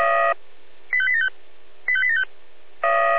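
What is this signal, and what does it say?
Five-tone selective call (ZVEI), heard as quick runs of five tones, tones of about 70 ms. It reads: 85875 85875
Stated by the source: Recorded with an SDR, ZVEI five-tone selective call.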